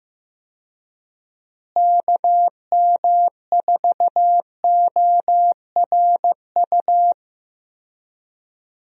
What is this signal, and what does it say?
Morse code, heard as a keyed tone, about 15 wpm, 700 Hz. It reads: KM4ORU
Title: USA Call Signs - 6 Characters - Repeated 2x - 15wpm